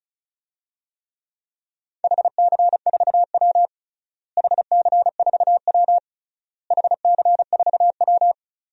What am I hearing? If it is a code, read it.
Morse code, heard as a keyed tone, about 36 wpm, 700 Hz. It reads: HC4W HC4W HC4W
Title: DXCC-100 Call Signs - 4 Characters - Repeated 3x - 35wpm